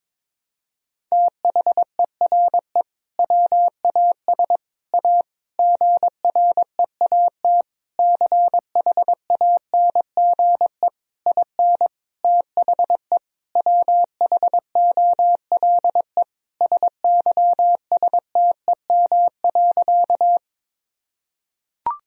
Morse code, heard as a keyed tone, about 21 wpm, 700 Hz. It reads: THERE WAS A GREAT CHANGE IN THE WHOLE SYSTEM.
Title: Sentences from Top 1000 Words - Review Part 3 - 22wpm